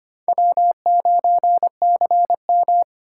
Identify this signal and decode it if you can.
Morse code, heard as a keyed tone, about 31 wpm, 700 Hz. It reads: W9CM
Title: Speed Racing - USA 1x2 Call Signs 25wpm